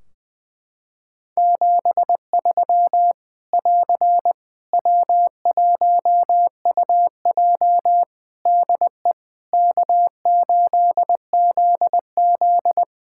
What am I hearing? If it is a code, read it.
Morse code, heard as a keyed tone, about 20 wpm, 700 Hz. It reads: 73 <AR> W1UJ DE K8ZZ
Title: Practice QSO - Exchange 4 - Ending - 20wpm